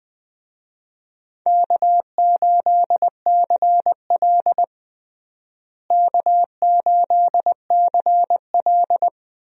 Morse code, heard as a keyed tone, about 20 wpm, 700 Hz. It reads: K8CL K8CL
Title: USA Call Signs - 4 Characters - Repeated 2x - 20wpm